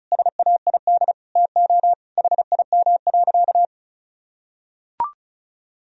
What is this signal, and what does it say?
Morse code, heard as a keyed tone, about 36 wpm, 700 Hz. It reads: SAID TO HIM.
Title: Sentences from Top 1000 Words - Review Part 1 - 35wpm